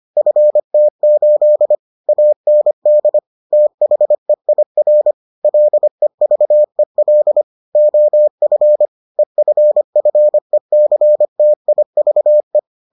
Morse code, heard as a keyed tone, about 25 wpm, 600 Hz. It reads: FT8 AND THEIR LEVEL OF EFFECTIVE